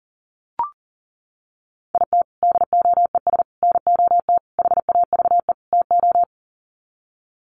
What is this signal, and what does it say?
Morse code, heard as a keyed tone, about 38 wpm, 700 Hz. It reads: IT DOES NOT HAVE TO
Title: Sets of 5 Words 40wpm